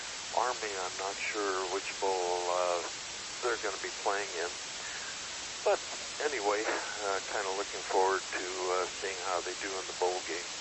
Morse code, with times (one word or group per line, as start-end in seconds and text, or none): none